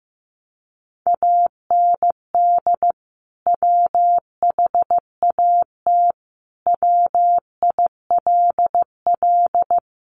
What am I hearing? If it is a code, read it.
Morse code, heard as a keyed tone, about 15 wpm, 700 Hz. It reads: AND WHAT WILL
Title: Sentences from Top 600 Words 15wpm